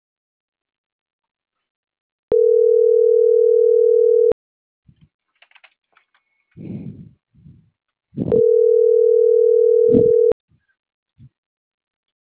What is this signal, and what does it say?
Ringback tone (North America), 440 + 480 Hz.